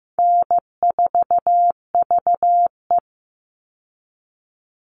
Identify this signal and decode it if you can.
Morse code, heard as a keyed tone, about 15 wpm, 700 Hz. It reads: N4VE
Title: All Cut Numbers - 4-Digit 15wpm